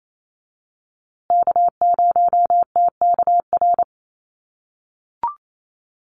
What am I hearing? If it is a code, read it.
Morse code, heard as a keyed tone, about 28 wpm, 700 Hz. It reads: K0TKR
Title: USA Call Signs Repeated 2x 28wpm